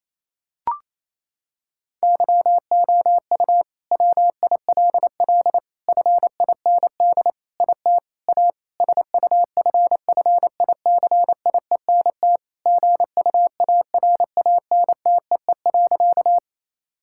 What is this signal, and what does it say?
Morse code, heard as a keyed tone, about 28 wpm, 700 Hz. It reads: YOU WILL FIND IT A SUFFICIENT GUARANTEE.
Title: Sentences from Top 1600 Words 28wpm